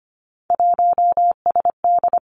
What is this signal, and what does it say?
Morse code, heard as a keyed tone, about 25 wpm, 700 Hz. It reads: 1SD
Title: Three Challenging Letter-Number Combinations 25wpm